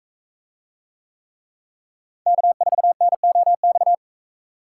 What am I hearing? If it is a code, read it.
Morse code, heard as a keyed tone, about 38 wpm, 700 Hz. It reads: K4NOX